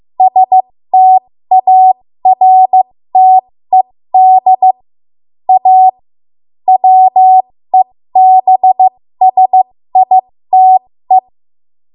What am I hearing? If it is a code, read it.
Morse code, heard as a keyed tone, about 15 wpm, spaced as for 13 wpm, 750 Hz. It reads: STARTED A WEBSITE
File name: Morse / 13WPM